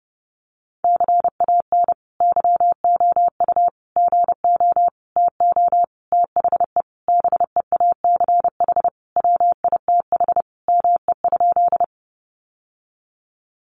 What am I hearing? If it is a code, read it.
Morse code, heard as a keyed tone, about 30 wpm, 700 Hz. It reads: CAN YOU GO TO THE BEACH WITH ME?
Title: Sentences from Top 1500 Words 30wpm